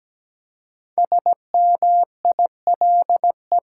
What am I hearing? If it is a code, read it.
Morse code, heard as a keyed tone, about 17 wpm, 700 Hz. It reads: SMILE
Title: Five Letter Words - 17wpm